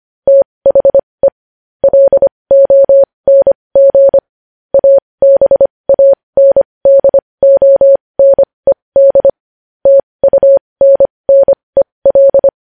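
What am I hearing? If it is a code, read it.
Morse code, heard as a keyed tone, about 25 wpm, spaced as for 20 wpm, 550 Hz. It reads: THE LONG ABANDONED TUNNEL
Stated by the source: Morse code news headlines practice at 20.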